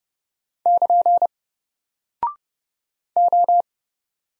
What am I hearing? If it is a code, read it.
Morse code, heard as a keyed tone, about 30 wpm, 700 Hz. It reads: <KN> O
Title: Single Character (Letter:Number:Punctuation:Prosign) 30wpm